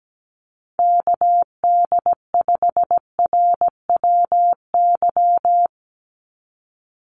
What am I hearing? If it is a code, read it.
Morse code, heard as a keyed tone, about 17 wpm, 700 Hz. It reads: KD5RWY
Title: USA Call Signs 17wpm